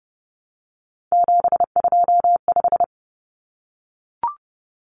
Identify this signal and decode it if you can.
Morse code, heard as a keyed tone, about 30 wpm, 700 Hz. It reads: 725
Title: Common Cut Numbers - 3-Digit 30wpm